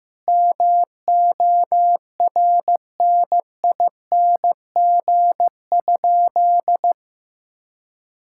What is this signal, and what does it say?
Morse code, heard as a keyed tone, about 15 wpm, 700 Hz. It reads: MORNING?
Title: Sentences from Top 1600 Words 15wpm